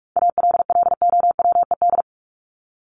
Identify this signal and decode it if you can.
Morse code, heard as a keyed tone, about 44 wpm, 700 Hz. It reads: ALLOWED
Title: ED Suffix - Top 100 Words with Base Words 45wpm